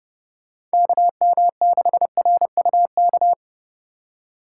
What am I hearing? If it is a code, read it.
Morse code, heard as a keyed tone, about 34 wpm, 700 Hz. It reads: KM6RUK